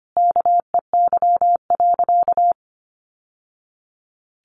Morse code, heard as a keyed tone, about 25 wpm, 700 Hz. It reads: KEY.
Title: Sentences from Top 600 Words 25wpm